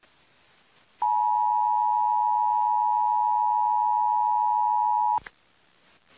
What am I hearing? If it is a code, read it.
Emergency Alert System (SAME) tones (the 853 and 960 Hz attention tone). It is heard from 1.0 to 5.2 s.
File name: eas / KUTTKGMT / KUTTKGMT